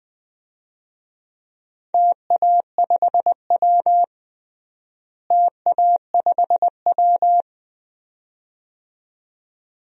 Morse code, heard as a keyed tone, about 20 wpm, 700 Hz. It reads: TA5W TA5W
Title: DXCC-100 Call Signs - 4 Characters - Repeated 2x - 20wpm